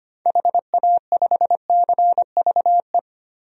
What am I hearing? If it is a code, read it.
Morse code, heard as a keyed tone, about 25 wpm, 700 Hz. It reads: HA5CVE
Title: DXCC-100 Call Signs - 6 Characters - Repeated 2x - 25wpm